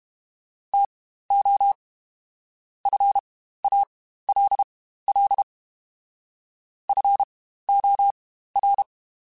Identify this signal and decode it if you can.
Morse code, heard as a keyed tone, about 32 wpm, spaced as for 15 wpm, 800 Hz. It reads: TO FALL FOR
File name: Quotes Spaced / Quotes Spaced-mp3s / StandFor sp @30